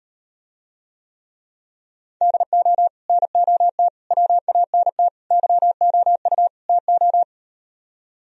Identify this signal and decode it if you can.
Morse code, heard as a keyed tone, about 38 wpm, 700 Hz. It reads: DO NOT WANT YOU TO